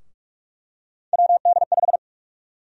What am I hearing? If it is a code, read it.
Morse code, heard as a keyed tone, about 44 wpm, 700 Hz. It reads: WD5
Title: USA Call Sign Prefixes - 45wpm